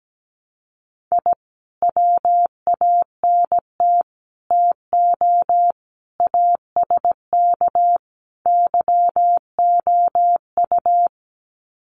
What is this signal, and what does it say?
Morse code, heard as a keyed tone, about 17 wpm, 700 Hz. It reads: I WANT TO ASK YOU
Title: Sets of 5 Words 17wpm